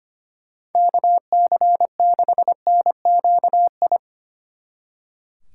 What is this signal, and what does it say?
Morse code, heard as a keyed tone, about 32 wpm, 700 Hz. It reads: KC6NQI